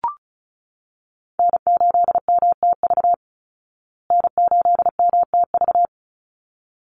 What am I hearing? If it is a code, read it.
Morse code, heard as a keyed tone, about 34 wpm, 700 Hz. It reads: N8MTV N8MTV
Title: Call Signs Repeated 3x - 35wpm